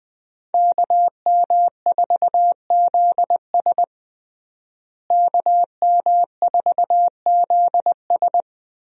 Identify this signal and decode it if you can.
Morse code, heard as a keyed tone, about 20 wpm, 700 Hz. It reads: KM4ZS KM4ZS
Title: USA Call Signs - 5 Characters - Repeated 2x - 20wpm